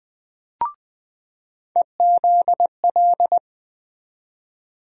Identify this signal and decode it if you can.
Morse code, heard as a keyed tone, about 20 wpm, 700 Hz. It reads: EZL
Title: USA Call Sign Postfixes - 3 Letters 20wpm